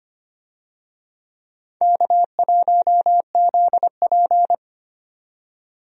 Morse code, heard as a keyed tone, about 25 wpm, 700 Hz. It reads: K1ZP
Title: USA Call Signs - 4 Characters - 25wpm